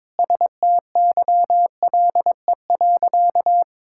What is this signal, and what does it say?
Morse code, heard as a keyed tone, about 21 wpm, 700 Hz. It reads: STYLE.
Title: Sentences from Top 1000 Words - Review Part 5 - 22wpm